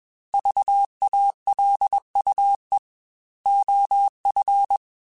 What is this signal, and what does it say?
Morse code, heard as a keyed tone, about 21 wpm, 800 Hz. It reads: VALUE OF
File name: Technical Subjects / Technical Subjects-mp3s / XC @20